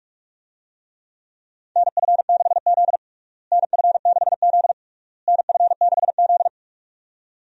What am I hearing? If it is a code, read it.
Morse code, heard as a keyed tone, about 46 wpm, 700 Hz. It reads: NF6Z NF6Z NF6Z
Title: USA Call Signs - 4 Characters - Repeated 3x - 45wpm